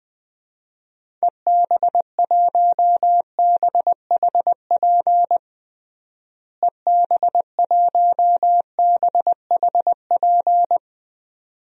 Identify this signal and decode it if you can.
Morse code, heard as a keyed tone, about 21 wpm, 700 Hz. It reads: EB1BHP EB1BHP
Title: DXCC-100 Call Signs - 6 Characters - Repeated 2x - 20wpm